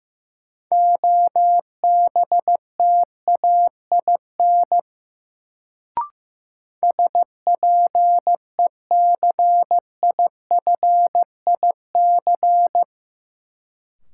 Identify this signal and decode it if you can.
Morse code, heard as a keyed tone, about 15 wpm, 700 Hz. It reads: OBTAIN SPECIFIC
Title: Sentences from Top 1300 Words - New Word Drill Alt - 15wpm